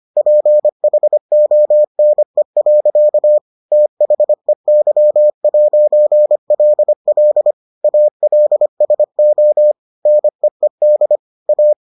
Morse code, heard as a keyed tone, about 25 wpm, 600 Hz. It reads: PHONE. THEY'LL ALSO NEED A